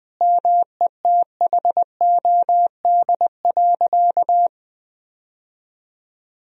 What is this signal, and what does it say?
Morse code, heard as a keyed tone, about 20 wpm, 700 Hz. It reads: METHOD.
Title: Sentences from Top 1000 Words - Review Part 3 - 20wpm